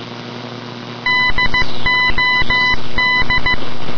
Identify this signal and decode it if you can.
Morse code, heard as a keyed tone, about 15 wpm, 2000 Hz. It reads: DOD
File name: dod / ambience / radio morse